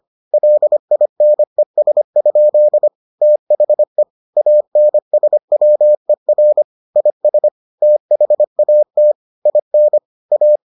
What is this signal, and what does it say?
Morse code, heard as a keyed tone, about 25 wpm, 600 Hz. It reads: LINES? THE ANSWER IS THAT IN A